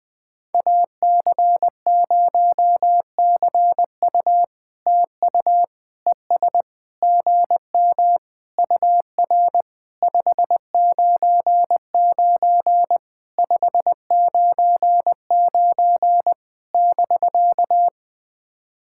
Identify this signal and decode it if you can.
Morse code, heard as a keyed tone, about 20 wpm, 700 Hz. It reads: AC0CU TU ES GM UR 599 599 <BK>
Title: POTA QSO Practice - 20wpm